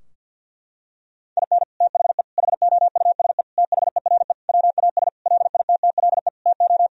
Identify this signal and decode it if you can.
Morse code, heard as a keyed tone, about 50 wpm, 700 Hz. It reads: IN THE HOUSE THERE WAS LITTLE TO
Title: Sentences from Top 1000 Words - Review Part 2 - 50wpm